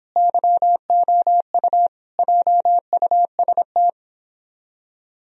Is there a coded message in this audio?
Yes: Morse code from 0.2 to 3.9 s, heard as a keyed tone, about 31 wpm, 700 Hz. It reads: YOU JUST